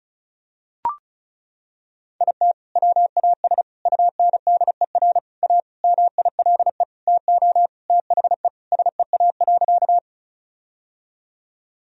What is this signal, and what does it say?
Morse code, heard as a keyed tone, about 34 wpm, 700 Hz. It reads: IT WAS UNDER A MILE TO THE SEA.
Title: Sentences from Top 1000 Words - Review Part 1 - 35wpm